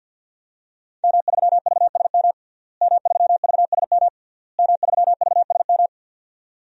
Morse code, heard as a keyed tone, about 51 wpm, 700 Hz. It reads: M3VSM M3VSM M3VSM
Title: DXCC-100 Call Signs - 5 Characters - Repeated 3x - 50wpm